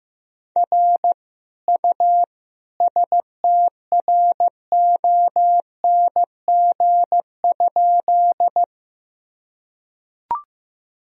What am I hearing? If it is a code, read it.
Morse code, heard as a keyed tone, about 15 wpm, 700 Hz. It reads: R U STRONG?
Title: Sentences from Top 500 Words and Abbreviations 15wpm